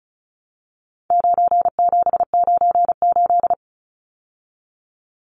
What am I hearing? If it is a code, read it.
Morse code, heard as a keyed tone, about 34 wpm, 700 Hz. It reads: 9798
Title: Numbers - 4-Digit 35wpm